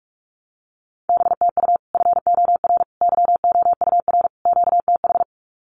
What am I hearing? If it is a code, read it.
Morse code, heard as a keyed tone, about 44 wpm, 700 Hz. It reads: BTU FOR YOUR QTH
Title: Common QSO Phrases 45wpm